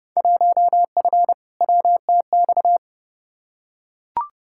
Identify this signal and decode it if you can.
Morse code, heard as a keyed tone, about 29 wpm, 700 Hz. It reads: 1F WTX
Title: Field Day Exchanges Encore - 30wpm